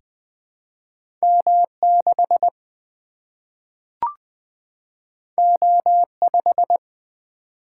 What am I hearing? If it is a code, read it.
Morse code, heard as a keyed tone, about 20 wpm, 700 Hz. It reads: M 6 O 5
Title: Single-Letter Single-Number 20wpm